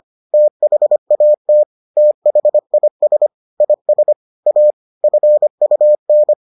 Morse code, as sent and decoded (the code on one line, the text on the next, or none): - .... .- - / - .... .. ... / .. ... / .- / ..-. ..- -.
THAT THIS IS A FUN